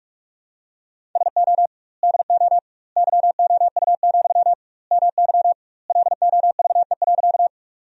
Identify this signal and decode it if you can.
Morse code, heard as a keyed tone, about 44 wpm, 700 Hz. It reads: SO DO YOU, MY LOVE.
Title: 2x Word-Spacing - Sentences from Top 400 Words - 45wpm